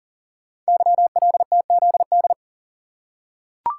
Morse code, heard as a keyed tone, about 42 wpm, 690 Hz. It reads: YLTZD
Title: Code Groups - 5 Letter:Numbers - 40wpm